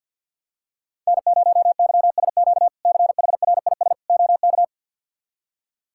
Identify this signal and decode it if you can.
Morse code, heard as a keyed tone, about 51 wpm, 700 Hz. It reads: N0YSQ CHRIS OK
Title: K1USN SST QSO Practice - 50wpm